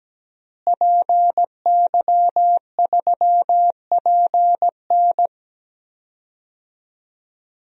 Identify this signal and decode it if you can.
Morse code, heard as a keyed tone, about 17 wpm, 700 Hz. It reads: PY3PN
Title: DXCC-100 Call Signs - 5 Characters - Repeated 2x - 17wpm